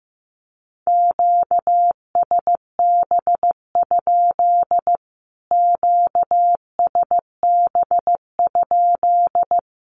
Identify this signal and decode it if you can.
Morse code, heard as a keyed tone, about 15 wpm, 700 Hz. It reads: QSB? QSB?
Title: Q-Codes Repeated 3x 15wpm